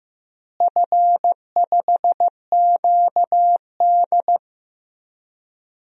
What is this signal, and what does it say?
Morse code, heard as a keyed tone, about 15 wpm, 700 Hz. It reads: F5QD
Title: DXCC-100 Call Signs - 4 Characters - Repeated 2x - 15wpm